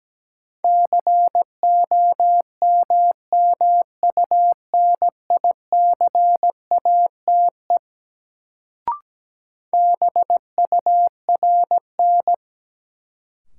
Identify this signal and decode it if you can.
Morse code, heard as a keyed tone, about 17 wpm, 700 Hz. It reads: COMMUNICATE BURN
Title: Sentences from Top 1300 Words - New Word Drill - 17wpm